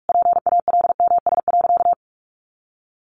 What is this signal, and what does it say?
Morse code, heard as a keyed tone, about 44 wpm, 700 Hz. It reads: PALMS.
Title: A Sound Of Thunder 45wpm